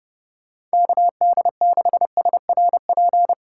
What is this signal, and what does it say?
Morse code, heard as a keyed tone, about 33 wpm, 700 Hz. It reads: KD6SRP